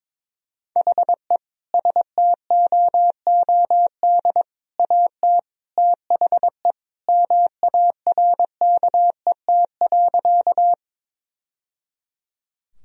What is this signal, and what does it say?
Morse code, heard as a keyed tone, about 22 wpm, 700 Hz. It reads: HE STOOD AT THE MARKET.
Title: Sentences from Top 1000 Words - Review Part 3 - 22wpm